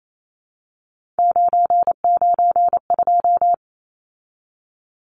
Morse code, heard as a keyed tone, about 28 wpm, 700 Hz. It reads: 992
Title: Numbers - 3-Digit 28wpm